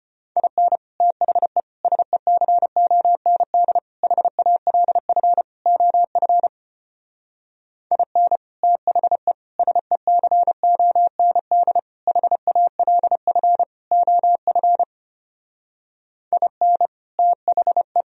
Morse code, heard as a keyed tone, about 33 wpm, 700 Hz. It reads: IN THE SECOND HALF OF IN THE SECOND HALF OF IN THE